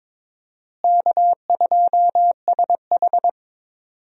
Morse code, heard as a keyed tone, about 22 wpm, 700 Hz. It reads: K2SH
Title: USA Call Signs - 4 Characters - Repeated 3x - 22wpm